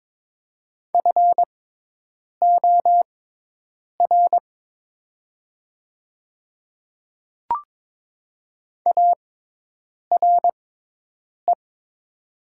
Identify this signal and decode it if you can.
Morse code, heard as a keyed tone, about 22 wpm, spaced as for 8 wpm, 700 Hz. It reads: FOR ARE
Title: 6x Character Spacing - Top 100 Words - 22wpm